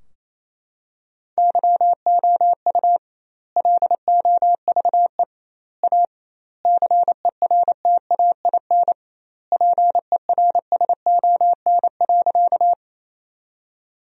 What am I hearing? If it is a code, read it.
Morse code, heard as a keyed tone, about 28 wpm, 700 Hz. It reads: YOU LOVE A CERTAIN PERSON.
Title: 2x Word-Spacing - Sentences from Top 500 Words - 28wpm